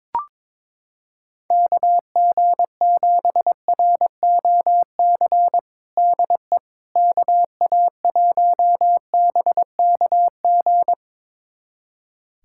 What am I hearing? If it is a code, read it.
Morse code, heard as a keyed tone, about 22 wpm, 700 Hz. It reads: KG7ROC DE KA1BKG
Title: USA Callsign DE Callsign - 6 Characters - 22wpm